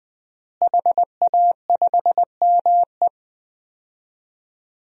Morse code, heard as a keyed tone, about 20 wpm, 700 Hz. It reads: HA5ME
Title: DXCC-100 Call Signs - 5 Characters - Repeated 3x - 20wpm